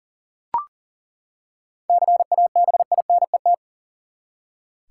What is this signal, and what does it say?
Morse code, heard as a keyed tone, about 40 wpm, 700 Hz. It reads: CABINET
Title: Seven Letter Words - 40wpm